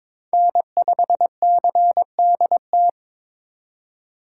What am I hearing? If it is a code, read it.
Morse code, heard as a keyed tone, about 26 wpm, 700 Hz. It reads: N5CDT
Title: Speed Racing - USA Call Signs 22wpm